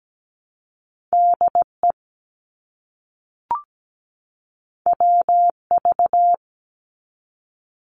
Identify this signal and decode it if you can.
Morse code, heard as a keyed tone, about 17 wpm, 700 Hz. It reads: DE WV
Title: US States 17wpm